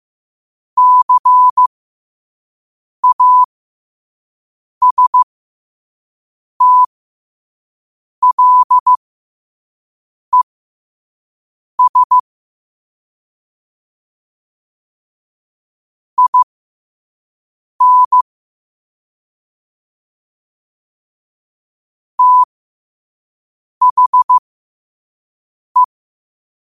Morse code, heard as a keyed tone, about 15 wpm, spaced as for 5 wpm, 1000 Hz. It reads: CASTLES IN THE
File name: qotdfiles / 5wpm